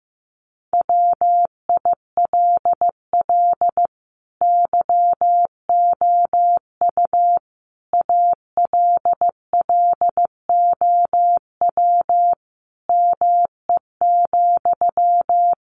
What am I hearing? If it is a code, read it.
Morse code, heard as a keyed tone, about 15 wpm, 700 Hz. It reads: WILL YOU ALLOW ME,
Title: Sentences from Top 700 Words 15wpm